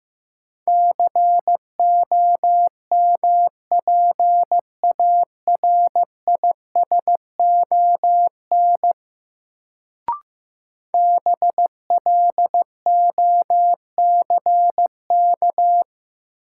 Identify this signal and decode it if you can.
Morse code, heard as a keyed tone, about 15 wpm, 700 Hz. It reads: COMPARISON BLOCK
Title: Sentences from Top 1600 Words - New Word Drill Alt - 15wpm